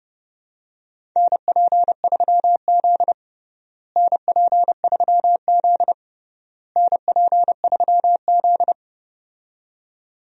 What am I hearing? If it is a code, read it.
Morse code, heard as a keyed tone, about 30 wpm, 700 Hz. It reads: NP3Z NP3Z NP3Z
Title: USA Call Signs - 4 Characters - Repeated 3x - 30wpm